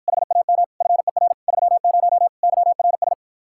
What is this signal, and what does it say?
Morse code, heard as a keyed tone, about 54 wpm, 700 Hz. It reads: HAM FER 30 YRS